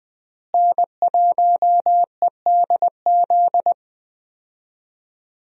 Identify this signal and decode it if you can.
Morse code, heard as a keyed tone, about 20 wpm, 700 Hz. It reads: N1EDZ
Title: USA Call Signs - 5 Characters - Repeated 3x - 20wpm